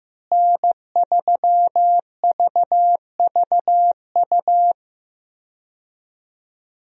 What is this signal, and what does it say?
Morse code, heard as a keyed tone, about 16 wpm, 700 Hz. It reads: N3VVU
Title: USA Call Signs - 5 Characters - Repeated 3x - 15wpm